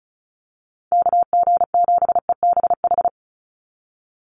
Morse code, heard as a keyed tone, about 34 wpm, 700 Hz. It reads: KG7EBH
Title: USA Call Signs 35wpm